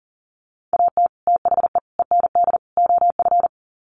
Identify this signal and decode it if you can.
Morse code, heard as a keyed tone, about 40 wpm, 700 Hz. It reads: AT THE END OF
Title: Sets of 4 Words 40wpm